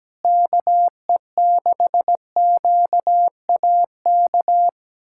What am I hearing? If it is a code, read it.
Morse code, heard as a keyed tone, about 17 wpm, 700 Hz. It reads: KE6QAK
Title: USA Callsign DE Callsign 17wpm